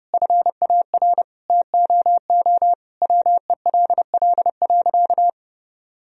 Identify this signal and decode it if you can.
Morse code, heard as a keyed tone, about 30 wpm, 700 Hz. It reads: FAR TOO WELL.
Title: Sentences from Top 1000 Words - Review Part 3 - 30wpm